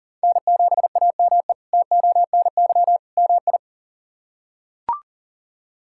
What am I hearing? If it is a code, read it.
Morse code, heard as a keyed tone, about 40 wpm, 690 Hz. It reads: N7AME TONY MI
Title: K1USN SST QSO Practice - 40wpm